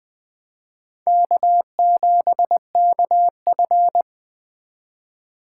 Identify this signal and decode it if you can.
Morse code, heard as a keyed tone, about 20 wpm, 700 Hz. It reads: K7KF
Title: Speed Racing - USA 1x2 Call Signs 17wpm